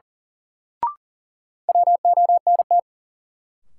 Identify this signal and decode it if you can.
Morse code, heard as a keyed tone, about 37 wpm, 700 Hz. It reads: WONT
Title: Sentences from Top 1000 Words - Review Part 6 - New Word Drill - 40wpm